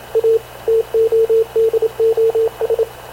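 Morse code, heard as a keyed tone, about 27 wpm, 450 Hz, with radio noise. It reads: A TODOS
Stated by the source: Havana Radio / CLA sending Spanish press (PX) in Morse, recorded by Radio Officer Paul Dumesnil onboard CNAV QUEST/CZDO.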